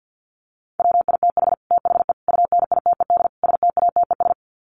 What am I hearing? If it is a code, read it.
Morse code, heard as a keyed tone, about 50 wpm, 700 Hz. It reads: WITH THE UNITED STATES
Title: Sets of 4 Words 50wpm